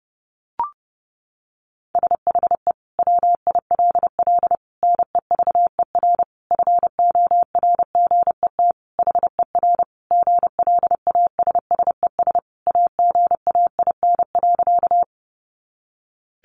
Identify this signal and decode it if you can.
Morse code, heard as a keyed tone, about 30 wpm, 700 Hz. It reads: SHE WILL NEVER FORGET HER GLASSES AGAIN.
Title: Additional Easy Phrases 30wpm